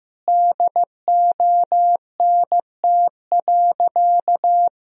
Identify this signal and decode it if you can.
Morse code, heard as a keyed tone, about 15 wpm, 700 Hz. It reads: DONT.
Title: Sentences from Top 1000 Words - Review Part 4 - 15wpm